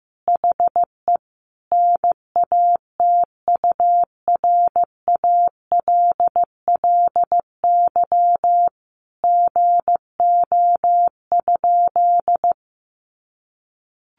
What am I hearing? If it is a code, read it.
Morse code, heard as a keyed tone, about 15 wpm, 700 Hz. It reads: HE NATURALLY GO?
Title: Sentences from Top 1300 Words 15wpm